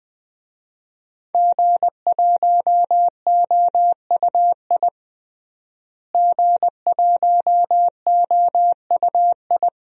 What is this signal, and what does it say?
Morse code, heard as a keyed tone, about 20 wpm, 700 Hz. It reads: G1OUI G1OUI
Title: DXCC-100 Call Signs - 5 Characters - Repeated 2x - 20wpm